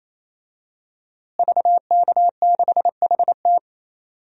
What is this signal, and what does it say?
Morse code, heard as a keyed tone, about 28 wpm, 700 Hz. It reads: VK6HT
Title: DXCC-100 Call Signs - 5 Characters - Repeated 3x - 28wpm